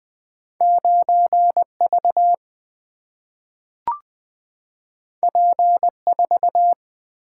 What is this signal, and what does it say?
Morse code, heard as a keyed tone, about 20 wpm, 700 Hz. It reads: 9 V P 4
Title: Single-Letter Single-Number 20wpm